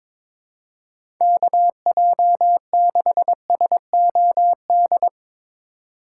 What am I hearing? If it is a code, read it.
Morse code, heard as a keyed tone, about 22 wpm, 700 Hz. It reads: KJ6SOD
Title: USA Call Signs - 6 Characters - 2x Character Spacing - 22wpm